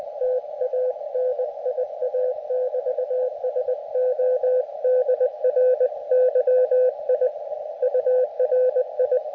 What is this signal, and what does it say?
Morse code, heard as a keyed tone, about 20 wpm, 510 Hz, with radio noise. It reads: TANIA<BT>SO DRY I URI
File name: Expedition / SWL UT5NM 20.12.2012 VANGUARD UA1CEG p IT9IFI